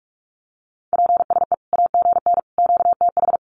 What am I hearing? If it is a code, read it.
Morse code, heard as a keyed tone, about 44 wpm, 700 Hz. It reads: PSE AGN QTH
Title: Common QSO Phrases 45wpm